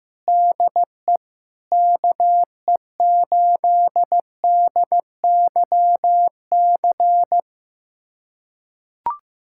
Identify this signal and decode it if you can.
Morse code, heard as a keyed tone, about 15 wpm, 700 Hz. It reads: DE KE8DYC
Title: USA Callsign DE Callsign - 6 Characters - 15wpm